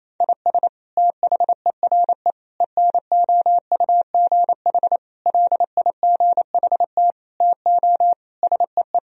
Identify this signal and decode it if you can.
Morse code, heard as a keyed tone, about 28 wpm, 700 Hz. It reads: IS THERE ENOUGH LIGHT TO SEE
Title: Sentences from Top 1000 Words - Review Part 1 - 28wpm